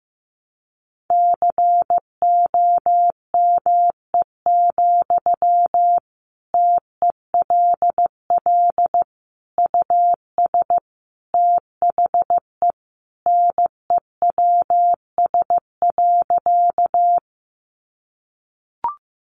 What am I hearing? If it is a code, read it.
Morse code, heard as a keyed tone, about 15 wpm, 700 Hz. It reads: COME, TELL US THE NEWS.
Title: Sentences from Top 800 Words 15wpm